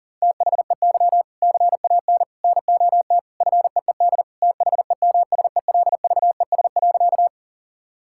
Morse code, heard as a keyed tone, about 40 wpm, 700 Hz. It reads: THEY CAN NOT FEED THEMSELVES.
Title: Sentences from Top 1600 Words 40wpm